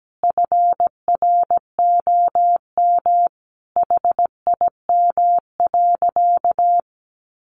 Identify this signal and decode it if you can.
Morse code, heard as a keyed tone, about 17 wpm, 700 Hz. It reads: FROM HIM.
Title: Sentences from Top 1200 Words 17wpm